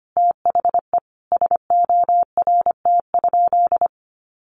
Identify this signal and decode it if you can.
Morse code, heard as a keyed tone, about 25 wpm, 700 Hz. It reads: THE SORT?
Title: Sentences from Top 1300 Words 25wpm